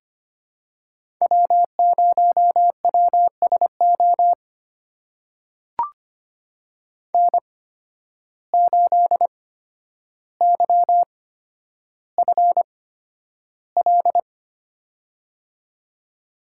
Morse code, heard as a keyed tone, about 25 wpm, spaced as for 11 wpm, 700 Hz. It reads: W0WSO N8YFL
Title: USA Call Signs - 5 Characters - 8x Character Spacing - 25wpm